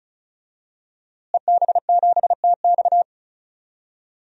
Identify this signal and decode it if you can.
Morse code, heard as a keyed tone, about 34 wpm, 700 Hz. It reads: EB7TX
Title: DXCC-100 Call Signs - 5 Characters - Repeated 2x - 35wpm